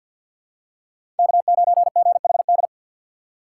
Morse code, heard as a keyed tone, about 47 wpm, 700 Hz. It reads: K9GHD
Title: USA Call Signs - 6x Character Spacing - 50wpm